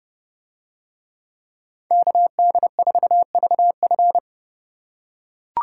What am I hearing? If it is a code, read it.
Morse code, heard as a keyed tone, about 27 wpm, 700 Hz. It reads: KD4VF